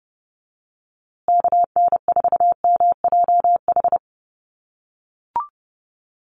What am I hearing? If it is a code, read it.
Morse code, heard as a keyed tone, about 30 wpm, 700 Hz. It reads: KN4MJH
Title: USA Call Signs Repeated 2x 30wpm